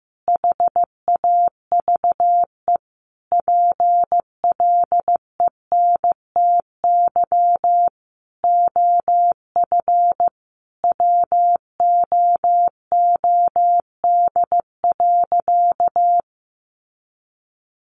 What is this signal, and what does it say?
Morse code, heard as a keyed tone, about 15 wpm, 700 Hz. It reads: HAVE PLENTY OF WOOD.
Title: Sentences from Top 1300 Words 15wpm